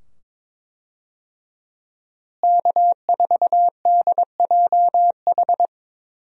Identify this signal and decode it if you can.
Morse code, heard as a keyed tone, about 22 wpm, 700 Hz. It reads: K4DJH
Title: USA Call Signs - 5 Characters - Repeated 2x - 22wpm